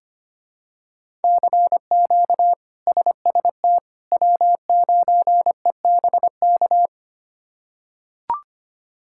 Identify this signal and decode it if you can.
Morse code, heard as a keyed tone, about 25 wpm, 700 Hz. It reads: CQ SST W9EBK
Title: K1USN SST QSO Practice - 25wpm